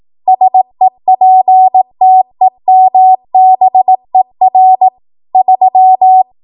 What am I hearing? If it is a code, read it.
Morse code, heard as a keyed tone, about 18 wpm, 750 Hz. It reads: SEPTEMBER 3
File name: Archive / 18 WPM / 18WPM